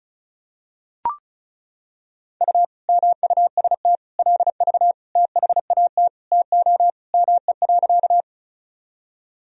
Morse code, heard as a keyed tone, about 36 wpm, 700 Hz. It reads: U MUST LV THAT TO ME.
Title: Sentences from Top 400 Words and Abbreviations 35wpm